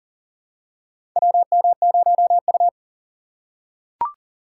Morse code, heard as a keyed tone, about 39 wpm, 700 Hz. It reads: WM0U
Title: USA Call Signs - 4 Characters - 4x Character Spacing - 40wpm